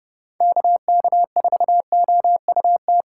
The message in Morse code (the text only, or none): KK4OUT